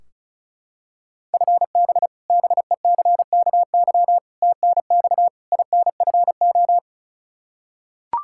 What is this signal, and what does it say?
Morse code, heard as a keyed tone, about 34 wpm, 700 Hz. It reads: FB BECKY TNX INFO
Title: Practice QSO - Exchange 3 - More Info - 35wpm